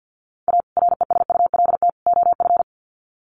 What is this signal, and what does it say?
Morse code, heard as a keyed tone, about 50 wpm, 700 Hz. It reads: A RESULT OF
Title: Sets of 3 Words 50wpm